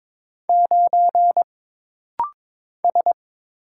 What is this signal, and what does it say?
Morse code, heard as a keyed tone, about 22 wpm, 700 Hz. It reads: 9 S
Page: This rapid-fire timing is geared to push your ICR (Instant Character Recognition) skills.